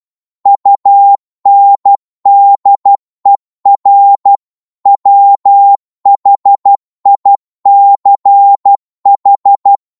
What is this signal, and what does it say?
Morse code, heard as a keyed tone, about 12 wpm, 800 Hz. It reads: UNDER WHICH